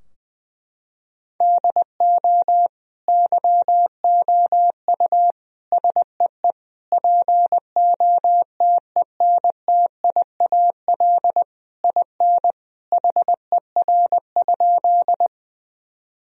Morse code, heard as a keyed tone, about 20 wpm, 700 Hz. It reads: DO YOU SEE POTENTIAL IN HER?
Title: Sentences from Top 1000 Words - Review Part 4 - 20wpm